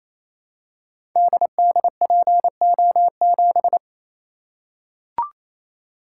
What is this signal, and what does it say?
Morse code, heard as a keyed tone, about 27 wpm, 700 Hz. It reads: DDPO7
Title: Code Groups - 5 Letter:Numbers - 28wpm